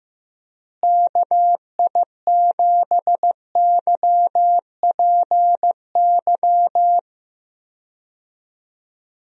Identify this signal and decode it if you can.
Morse code, heard as a keyed tone, about 15 wpm, 700 Hz. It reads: KI7YPY
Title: USA Call Signs - 6 Characters - Repeated 2x - 15wpm